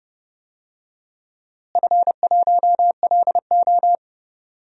Morse code, heard as a keyed tone, about 30 wpm, 700 Hz. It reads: F1LO
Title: DXCC-100 Call Signs - 4 Characters - Repeated 2x - 30wpm